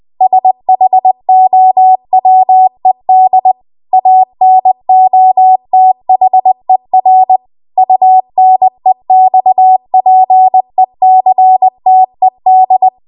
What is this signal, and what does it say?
Morse code, heard as a keyed tone, about 20 wpm, 750 Hz. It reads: SHOWED ANOTHER UNEXPECTED
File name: Archive / 20 WPM / 20WPM